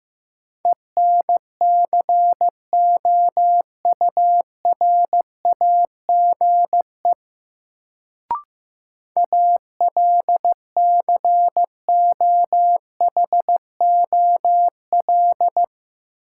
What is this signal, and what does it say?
Morse code, heard as a keyed tone, about 15 wpm, 700 Hz. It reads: ENCOURAGE ALCOHOL
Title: Sentences from Top 1400 Words - New Word Drill - 15wpm